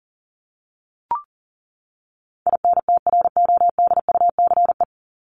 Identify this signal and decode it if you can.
Morse code, heard as a keyed tone, about 40 wpm, 700 Hz. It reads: INTRODUCE
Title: Top 1501-1750 Words 40wpm